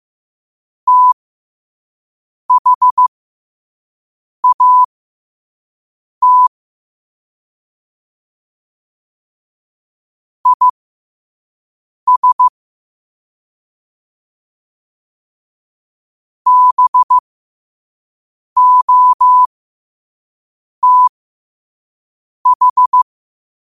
Morse code, heard as a keyed tone, about 15 wpm, spaced as for 5 wpm, 1000 Hz. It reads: THAT IS BOTH
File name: qotdfiles / 5wpm